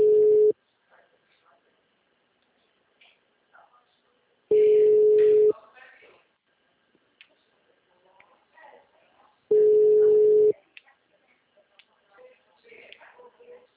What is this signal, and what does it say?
Ringback tone (Europe, 425 Hz), with speech.
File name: out- -2030